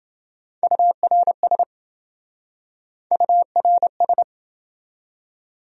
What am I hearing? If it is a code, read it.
Morse code, heard as a keyed tone, about 28 wpm, 700 Hz. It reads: URS URS